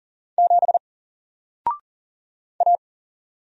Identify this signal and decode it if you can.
Morse code, heard as a keyed tone, about 40 wpm, 700 Hz. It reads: 7 A
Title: Single Letter-Number Alt 40wpm